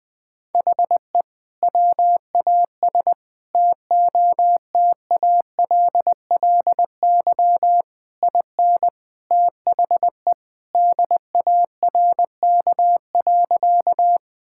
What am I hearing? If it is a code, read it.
Morse code, heard as a keyed tone, about 20 wpm, 700 Hz. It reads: HE WAS TOTALLY IN THE DARK.
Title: Sentences from Top 1600 Words 20wpm